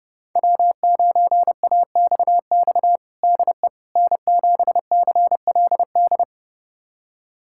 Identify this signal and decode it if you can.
Morse code, heard as a keyed tone, about 32 wpm, 700 Hz. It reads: W9AXX DE N7CLD
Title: USA Callsign DE Callsign - 5 Characters - 30wpm